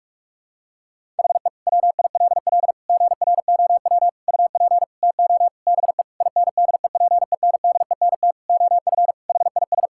Morse code, heard as a keyed tone, about 46 wpm, 700 Hz. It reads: HE WILL GROW UP TO BE INDEPENDENT OF HIS
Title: Sentences from Top 1000 Words - Review Part 6 - 45wpm